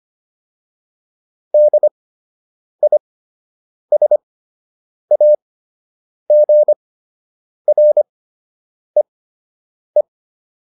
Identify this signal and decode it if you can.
Morse code, heard as a keyed tone, about 25 wpm, spaced as for 8 wpm, 600 Hz. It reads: DISAGREE